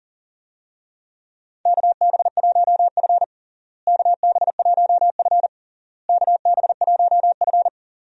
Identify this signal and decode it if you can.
Morse code, heard as a keyed tone, about 40 wpm, 700 Hz. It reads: KB1F KB1F KB1F
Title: USA Call Signs - 4 Characters - Repeated 3x - 40wpm